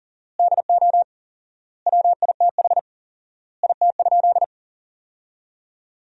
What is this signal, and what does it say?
Morse code, heard as a keyed tone, about 37 wpm, 700 Hz. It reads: DO WITH IT?
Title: 4x Word-Spacing - Sentences from Top 100 Words - 40wpm